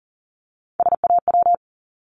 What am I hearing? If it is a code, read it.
Morse code, heard as a keyed tone, about 41 wpm, 690 Hz. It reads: SAW
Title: Multi-letter Phonemes 40wpm